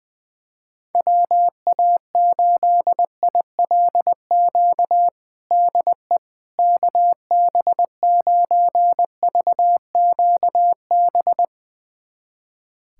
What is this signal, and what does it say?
Morse code, heard as a keyed tone, about 21 wpm, 700 Hz. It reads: WA8ILQ DE KB9VQB
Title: USA Callsign DE Callsign 20wpm